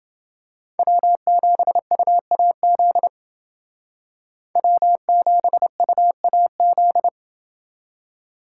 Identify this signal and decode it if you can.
Morse code, heard as a keyed tone, about 28 wpm, 700 Hz. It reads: W7UAZ W7UAZ